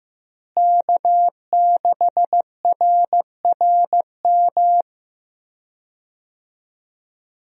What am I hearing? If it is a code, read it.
Morse code, heard as a keyed tone, about 15 wpm, 700 Hz. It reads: K6RRM
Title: USA Call Signs - 5 Characters - Repeated 2x - 15wpm